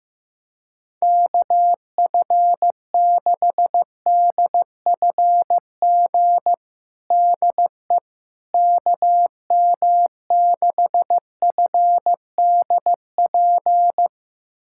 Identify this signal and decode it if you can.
Morse code, heard as a keyed tone, about 15 wpm, 700 Hz. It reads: KF6DFG DE KM6FDP
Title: USA Callsign DE Callsign - 6 Characters - 15wpm